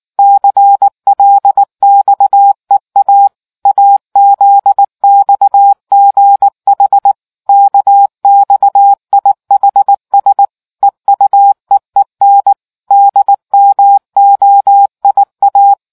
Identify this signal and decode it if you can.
Morse code, heard as a keyed tone, about 19 wpm, 800 Hz. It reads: CLXEA AZXGH KXIHS EUEEN DMOIA